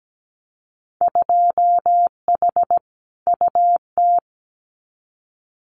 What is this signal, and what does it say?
Morse code, heard as a keyed tone, about 17 wpm, 700 Hz. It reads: 2H UT
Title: Winter Field Day Exchanges Encore 17wpm